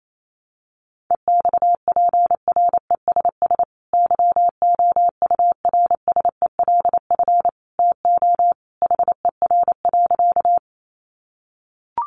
Morse code, heard as a keyed tone, about 28 wpm, 700 Hz. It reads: EXPRESS YOURSELF TO HER.
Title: Sentences from Top 1100 Words 28wpm